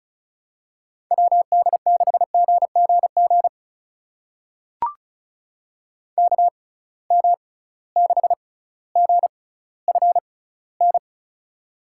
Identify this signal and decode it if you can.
Morse code, heard as a keyed tone, about 36 wpm, spaced as for 18 wpm, 700 Hz. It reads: WD6GGG KM6GFN
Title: USA Call Signs - 6 Characters - 6x Character Spacing - 35wpm